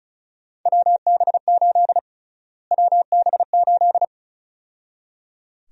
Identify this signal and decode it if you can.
Morse code, heard as a keyed tone, about 36 wpm, 700 Hz. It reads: WB8 WB8
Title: USA Call Sign Prefixes - Repeated 2x - 35wpm